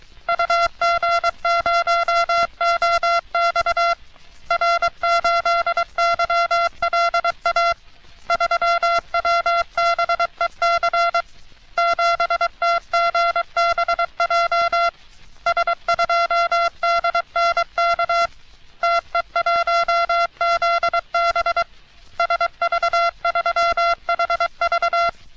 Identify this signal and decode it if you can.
Morse code, heard as a keyed tone, about 23 wpm, 680 Hz. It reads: UG0OX R8YLA 3WBEC 7TGBJ S2DNK TE1ZB SV3HV
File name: U.S. Navy - RN-2849-124 - Sixteens Morse Code Practice